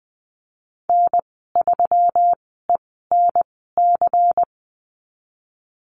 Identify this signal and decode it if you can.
Morse code, heard as a keyed tone, about 20 wpm, spaced as for 15 wpm, 700 Hz. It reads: N3ENC
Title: 154 Lesson 25g - Letter p - Call Signs 20wpm